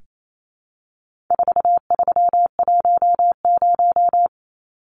Morse code, heard as a keyed tone, about 28 wpm, 700 Hz. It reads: 4310
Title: Numbers - 4-Digit 28wpm